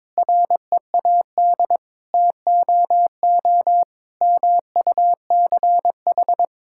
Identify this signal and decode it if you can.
Morse code, heard as a keyed tone, about 22 wpm, 700 Hz. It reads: READ TOO MUCH
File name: Sets of 9 Words - 22wpm